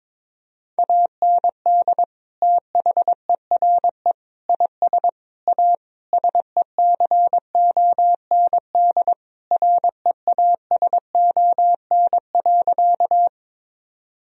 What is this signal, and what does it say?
Morse code, heard as a keyed tone, about 22 wpm, 700 Hz. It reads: AND THERE IS A SECOND REASON.
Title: Sentences from Top 1000 Words - Review Part 1 - 22wpm